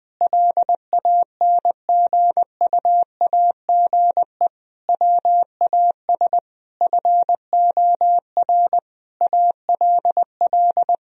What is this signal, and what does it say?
Morse code, heard as a keyed tone, about 21 wpm, 700 Hz. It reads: LANGUAGE WAS FOR ALL
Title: Sentences from Top 1000 Words - Review Part 4 - 20wpm